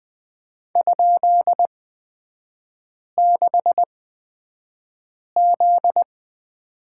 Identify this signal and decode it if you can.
Morse code, heard as a keyed tone, about 20 wpm, spaced as for 5 wpm, 700 Hz. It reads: ?6Z
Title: 243 Lesson 39e - Letter z - Three Characters 20wpm